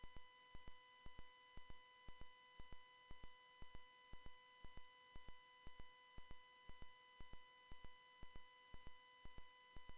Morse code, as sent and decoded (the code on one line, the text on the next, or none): none